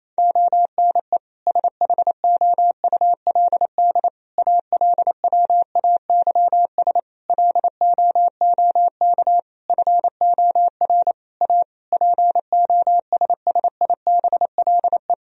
Morse code, heard as a keyed tone, about 27 wpm, 700 Hz. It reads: ONE SHOULD ALWAYS LOOK FOR A POSSIBLE
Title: Sentences from Top 1000 Words - Review Part 4 - 28wpm